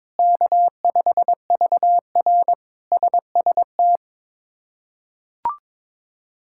Morse code, heard as a keyed tone, about 22 wpm, 700 Hz. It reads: K5VR SST
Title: K1USN SST QSO Practice - 22wpm